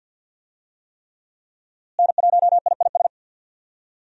Morse code, heard as a keyed tone, about 52 wpm, 700 Hz. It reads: N1IIS
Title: USA Call Signs - 5 Characters - Repeated 3x - 50wpm